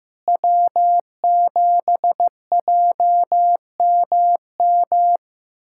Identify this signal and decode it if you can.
Morse code, heard as a keyed tone, about 15 wpm, 700 Hz. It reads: W7JMM
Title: K1USN SST QSO Practice - 15wpm